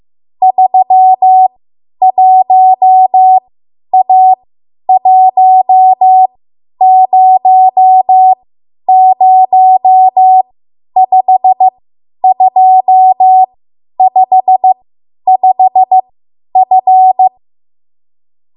Morse code, heard as a keyed tone, about 15 wpm, spaced as for 10 wpm, 750 Hz. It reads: 31A1005255F
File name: Morse / 10WPM